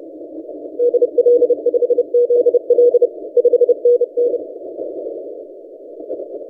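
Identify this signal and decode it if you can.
Morse code, heard as a keyed tone, about 28 wpm, 480 Hz. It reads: DL5ZL 5NN
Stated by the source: ZD9XF DX station heard in CW on 30 m, recorded by N6ML.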